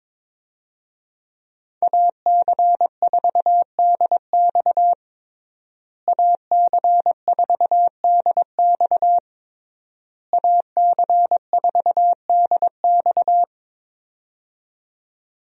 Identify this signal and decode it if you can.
Morse code, heard as a keyed tone, about 22 wpm, 700 Hz. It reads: AC4DX AC4DX AC4DX
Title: USA Call Signs - 5 Characters - Repeated 3x - 22wpm